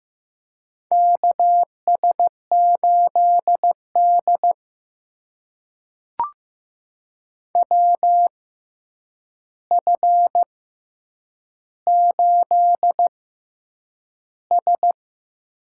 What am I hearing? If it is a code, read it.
Morse code, heard as a keyed tone, about 15 wpm, spaced as for 8 wpm, 700 Hz. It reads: KS8D WF8S
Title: USA Call Signs - 4 Characters - 6x Character Spacing - 15wpm